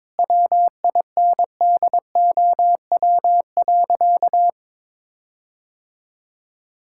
Morse code, heard as a keyed tone, about 21 wpm, 700 Hz. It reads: WINDOW.
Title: Sentences from Top 1600 Words 22wpm